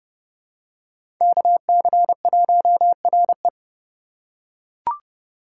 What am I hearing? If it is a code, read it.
Morse code, heard as a keyed tone, about 30 wpm, 700 Hz. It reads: KC1RE
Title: USA Call Signs - 5 Characters - 2x Character Spacing - 30wpm